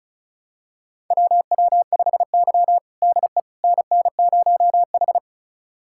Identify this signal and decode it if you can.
Morse code, heard as a keyed tone, about 36 wpm, 700 Hz. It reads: WW5Y DE NN0H
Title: USA Callsign DE Callsign - 4 Characters - 35wpm